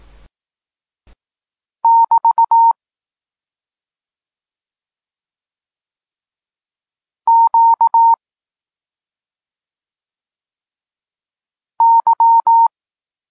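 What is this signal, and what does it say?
Morse code, heard as a keyed tone, about 18 wpm, 920 Hz. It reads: <BT> Q Y
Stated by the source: Polish Morse lesson teaching the letter C, from SP9MOA Niepołomicki Klub Krótkofalowców.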